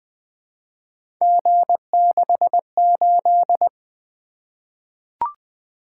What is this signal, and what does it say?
Morse code, heard as a keyed tone, about 20 wpm, 700 Hz. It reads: G68
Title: Three Letter-Number Combinations - 20wpm